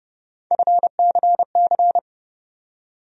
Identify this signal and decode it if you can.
Morse code, heard as a keyed tone, about 30 wpm, 700 Hz. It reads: FCC
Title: USA Call Sign Postfixes - 3 Letters 30wpm